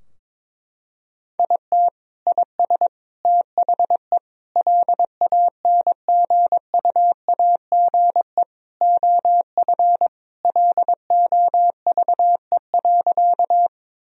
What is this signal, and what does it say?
Morse code, heard as a keyed tone, about 22 wpm, 700 Hz. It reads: IT IS THE LANGUAGE OF LOVE.
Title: Sentences from Top 1000 Words - Review Part 3 - 22wpm